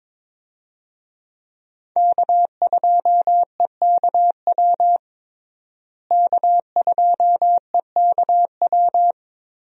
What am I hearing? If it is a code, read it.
Morse code, heard as a keyed tone, about 22 wpm, 700 Hz. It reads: K2EKW K2EKW
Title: USA Call Signs - 5 Characters - Repeated 3x - 22wpm